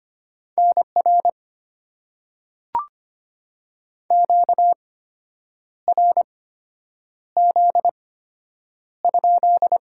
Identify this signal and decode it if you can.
Morse code, heard as a keyed tone, about 25 wpm, spaced as for 8 wpm, 700 Hz. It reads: NR QRZ?
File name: Top 90 QSO Elements - 8x Character Spacing - 25wpm